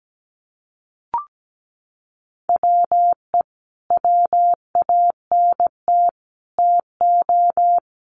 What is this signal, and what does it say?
Morse code, heard as a keyed tone, about 17 wpm, 700 Hz. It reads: WE WANT TO
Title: Sets of 3 Words 17wpm